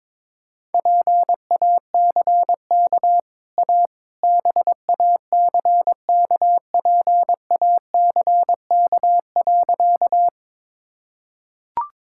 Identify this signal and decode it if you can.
Morse code, heard as a keyed tone, about 22 wpm, 690 Hz. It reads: PACK A BACKPACK.
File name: practice / 22wpm / Easy Sentences with Hints - 22wpm